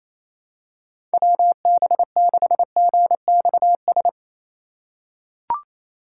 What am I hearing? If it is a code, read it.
Morse code, heard as a keyed tone, about 28 wpm, 700 Hz. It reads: WB6GXS
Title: USA Call Signs - 6 Characters - 3x Character Spacing - 28wpm